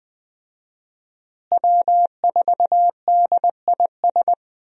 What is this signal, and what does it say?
Morse code, heard as a keyed tone, about 21 wpm, 700 Hz. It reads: W4DIS
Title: USA Call Signs - 5 Characters - Repeated 3x - 20wpm